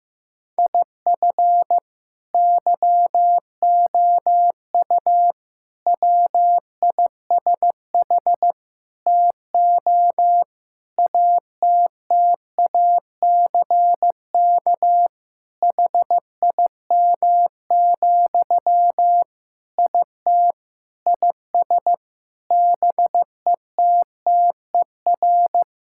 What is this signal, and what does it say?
Morse code, heard as a keyed tone, about 15 wpm, 700 Hz. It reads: IF YOU WISH TO ATTACK HIM, IT IS BETTER
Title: Sentences from Top 1600 Words 15wpm